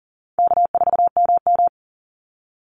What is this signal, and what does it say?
Morse code, heard as a keyed tone, about 38 wpm, 700 Hz. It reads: K4MM
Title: Easy Call Signs - 40wpm